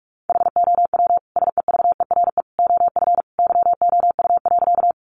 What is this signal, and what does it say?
Morse code, heard as a keyed tone, about 44 wpm, 700 Hz. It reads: HOW SEVERE OF YOU.
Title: Sentences from Top 1400 Words 45wpm